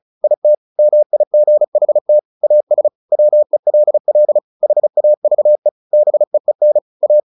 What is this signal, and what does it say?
Morse code, heard as a keyed tone, about 34 wpm, 600 Hz. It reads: IT MIGHT AS WELL HAVE BEEN A
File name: the old man and the sea 2 35wpm